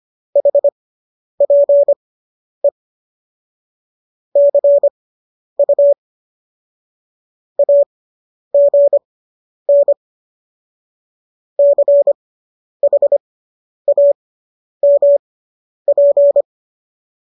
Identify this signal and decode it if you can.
Morse code, heard as a keyed tone, about 25 wpm, spaced as for 10 wpm, 570 Hz. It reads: HPE CU AGN CHAMP
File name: qso005 10